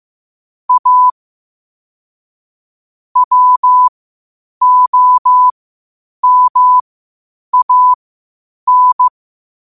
Morse code, heard as a keyed tone, about 15 wpm, spaced as for 8 wpm, 1000 Hz. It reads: A WOMAN
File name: qotdfiles / 8wpm